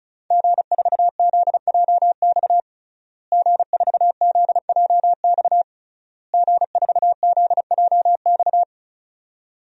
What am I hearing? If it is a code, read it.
Morse code, heard as a keyed tone, about 36 wpm, 700 Hz. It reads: G4ZJX G4ZJX G4ZJX
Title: DXCC-100 Call Signs - 5 Characters - Repeated 3x - 35wpm